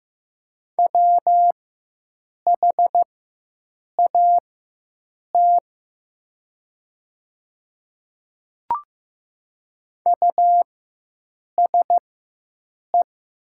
Morse code, heard as a keyed tone, about 15 wpm, spaced as for 7 wpm, 700 Hz. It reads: WHAT USE
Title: 4x Character Spacing - Top 100 Words - 15wpm